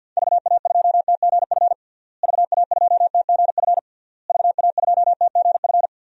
Morse code, heard as a keyed tone, about 48 wpm, 700 Hz. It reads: VA2TGF VA2TGF VA2TGF
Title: DXCC-100 Call Signs - 6 Characters - Repeated 3x - 50wpm